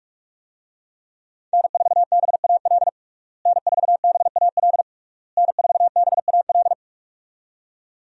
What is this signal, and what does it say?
Morse code, heard as a keyed tone, about 46 wpm, 700 Hz. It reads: N4BAL N4BAL N4BAL
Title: USA Call Signs - 5 Characters - Repeated 3x - 45wpm